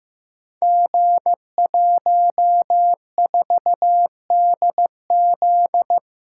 Morse code, heard as a keyed tone, about 15 wpm, 700 Hz. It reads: G14DZ
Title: Code Groups - 5 Letter:Numbers - 15wpm